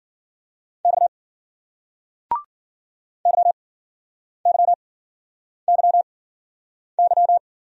Morse code, heard as a keyed tone, about 56 wpm, spaced as for 9 wpm, 700 Hz. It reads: X YYYY